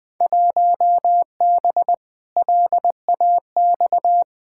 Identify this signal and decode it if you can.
Morse code, heard as a keyed tone, about 20 wpm, 700 Hz. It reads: 1B LAX
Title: Field Day Exchanges - 20wpm